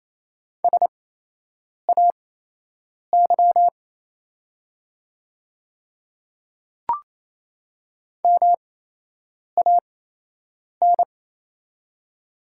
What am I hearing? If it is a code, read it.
Morse code, heard as a keyed tone, about 28 wpm, spaced as for 8 wpm, 700 Hz. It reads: SAY MAN
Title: 8x Character Spacing - Top 300 Words - 28wpm